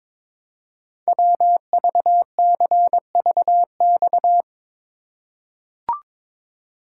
Morse code, heard as a keyed tone, about 22 wpm, 700 Hz. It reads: WVCVX
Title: Code Groups - 5 Letter:Numbers - 22wpm